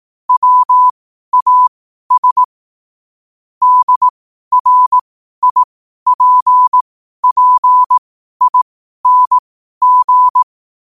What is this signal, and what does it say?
Morse code, heard as a keyed tone, about 18 wpm, spaced as for 12 wpm, 1000 Hz. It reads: WAS DRIPPING